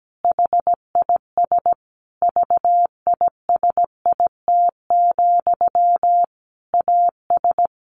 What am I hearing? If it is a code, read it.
Morse code, heard as a keyed tone, about 17 wpm, 700 Hz. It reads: HIS VISIT, AS
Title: Sentences from Top 1400 Words 17wpm